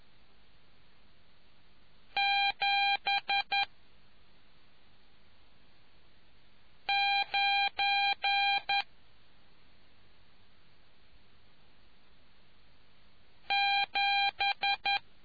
Morse code, heard as a keyed tone, about 11 wpm, spaced as for 2 wpm, 800 Hz. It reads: 79 7